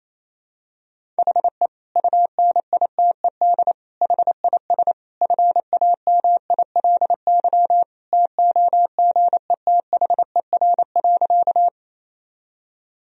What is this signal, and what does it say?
Morse code, heard as a keyed tone, about 28 wpm, 700 Hz. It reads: HE UNITED HIS FAMILY TOGETHER.
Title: Sentences from Top 1000 Words - Review Part 4 - 28wpm